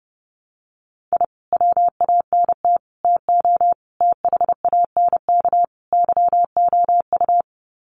Morse code, heard as a keyed tone, about 30 wpm, 700 Hz. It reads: I WANT TO THANK YOU
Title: Sets of 5 Words Encore 30wpm